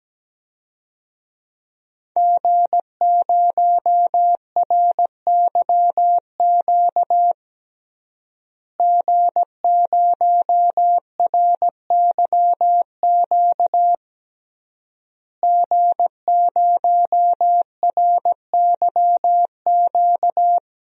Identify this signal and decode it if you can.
Morse code, heard as a keyed tone, about 17 wpm, 700 Hz. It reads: G0RYQ G0RYQ G0RYQ
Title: DXCC-100 Call Signs - 5 Characters - Repeated 3x - 17wpm